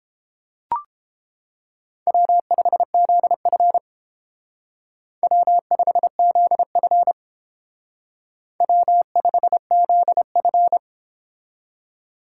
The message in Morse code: W5ZF W5ZF W5ZF

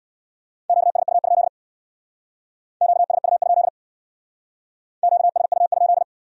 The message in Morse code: QSU? QSU? QSU?

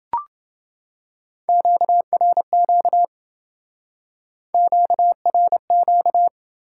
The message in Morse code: QRQ QRQ

